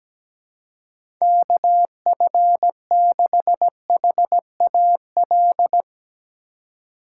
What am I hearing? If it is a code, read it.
Morse code, heard as a keyed tone, about 17 wpm, 700 Hz. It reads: KF6HAL
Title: USA Call Signs - 6 Characters - 3x Character Spacing - 17wpm